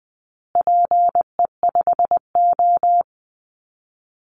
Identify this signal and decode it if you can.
Morse code, heard as a keyed tone, about 20 wpm, 700 Hz. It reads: PE5O
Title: Call Signs Repeated 3x - 20wpm